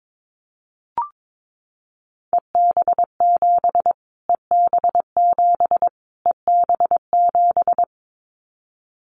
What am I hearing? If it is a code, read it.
Morse code, heard as a keyed tone, about 22 wpm, 700 Hz. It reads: EB7 EB7 EB7
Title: Callsign Prefixes for Top 25 DXCC - Repeated 3x - 22wpm